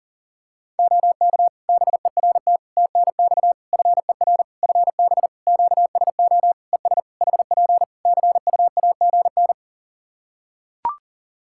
Morse code, heard as a keyed tone, about 37 wpm, 700 Hz. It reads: OK BERT TNX FER FB QSO ES HP CUAGN
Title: Practice QSO - Exchange 4 - Ending - 40wpm